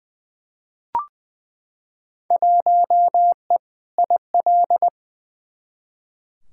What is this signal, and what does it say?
Morse code, heard as a keyed tone, about 21 wpm, 700 Hz. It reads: 1E IL
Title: Field Day Exchanges - 20wpm